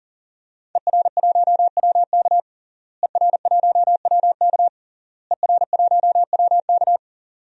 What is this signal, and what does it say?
Morse code, heard as a keyed tone, about 40 wpm, 700 Hz. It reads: ER1WK ER1WK ER1WK
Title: DXCC-100 Call Signs - 5 Characters - Repeated 3x - 40wpm